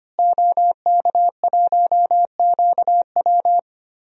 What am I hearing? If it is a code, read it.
Morse code, heard as a keyed tone, about 25 wpm, 700 Hz. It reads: OK1QW
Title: DXCC-100 Call Signs - 5 Characters - Repeated 2x - 25wpm